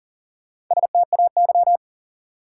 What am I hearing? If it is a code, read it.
Morse code, heard as a keyed tone, about 40 wpm, 700 Hz. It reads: STAY
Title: Sentences from Top 500 Words - New Word Drill - 40wpm